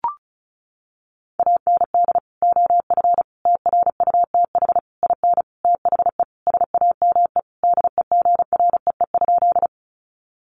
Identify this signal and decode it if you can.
Morse code, heard as a keyed tone, about 34 wpm, 700 Hz. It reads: AND OF TRUTH IN THE SAME DEGREE?
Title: Sentences from Top 1500 Words 35wpm